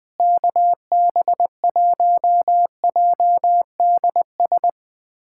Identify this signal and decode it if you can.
Morse code, heard as a keyed tone, about 20 wpm, 700 Hz. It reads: KB1JDS
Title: USA Callsign DE Callsign 20wpm